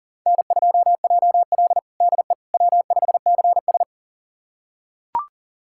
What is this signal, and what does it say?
Morse code, heard as a keyed tone, about 37 wpm, 700 Hz. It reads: N2JL DE W5CS
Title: USA Callsign DE Callsign - 4 Characters - 40wpm